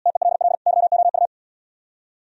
Morse code, heard as a keyed tone, about 69 wpm, 700 Hz. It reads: TELL YOU